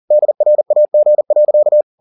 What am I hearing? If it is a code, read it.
Morse code, heard as a keyed tone, about 40 wpm, 600 Hz. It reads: DRAG.
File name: the old man and the sea 2 40wpm